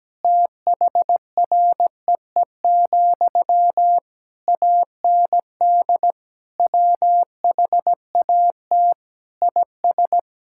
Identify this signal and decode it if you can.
Morse code, heard as a keyed tone, about 17 wpm, 700 Hz. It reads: THREE, AND WHAT IS
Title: Sentences from Top 1600 Words 17wpm